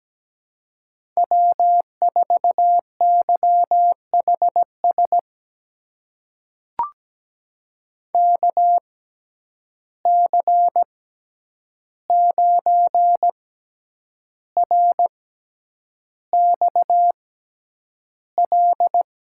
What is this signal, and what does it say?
Morse code, heard as a keyed tone, about 17 wpm, spaced as for 8 wpm, 700 Hz. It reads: W4YHS KC9RXL
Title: USA Call Signs - 6x Character Spacing - 17wpm